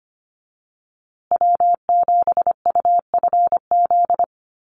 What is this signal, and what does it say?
Morse code, heard as a keyed tone, about 25 wpm, 700 Hz. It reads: W7UFZ
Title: USA Call Signs Repeated 3x 25wpm